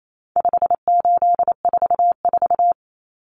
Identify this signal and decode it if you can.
Morse code, heard as a keyed tone, about 28 wpm, 700 Hz. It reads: 5844
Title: Common Cut Numbers - 4-Digit 28wpm